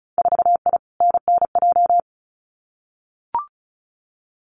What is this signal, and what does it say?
Morse code, heard as a keyed tone, about 34 wpm, 700 Hz. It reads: 4I NNJ
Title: Winter Field Day Exchanges 35wpm